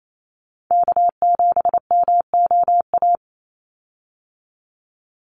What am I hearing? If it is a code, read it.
Morse code, heard as a keyed tone, about 28 wpm, 700 Hz. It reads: K7MOA
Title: Call Signs - 28wpm